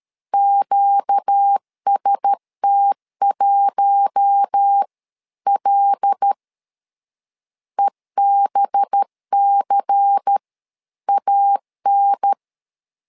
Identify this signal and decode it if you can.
Morse code, heard as a keyed tone, about 13 wpm, spaced as for 11 wpm, 800 Hz. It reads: QST1L EBCAN
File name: LEZIONI / 5 LBVC / 5 LBVC 12wpm 1955